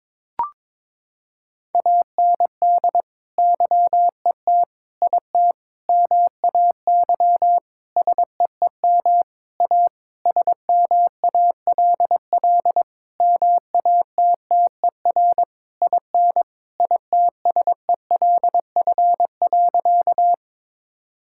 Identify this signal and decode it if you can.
Morse code, heard as a keyed tone, about 22 wpm, 700 Hz. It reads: AND YET IT MAY SEEM A SMALL MATTER IN ITSELF.
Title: Sentences from Top 1000 Words - Review Part 5 - 22wpm